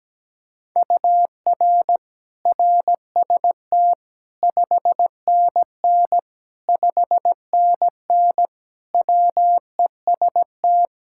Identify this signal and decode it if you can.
Morse code, heard as a keyed tone, about 17 wpm, 700 Hz. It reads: UR RST 5NN 5NN WEST
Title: Practice QSO - Exchange 1 - Beginning - 17wpm